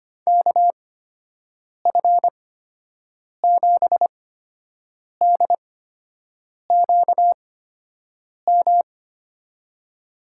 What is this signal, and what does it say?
Morse code, heard as a keyed tone, about 25 wpm, spaced as for 7 wpm, 700 Hz. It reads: KF7DQM
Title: USA Call Signs - 6 Characters - 8x Character Spacing - 25wpm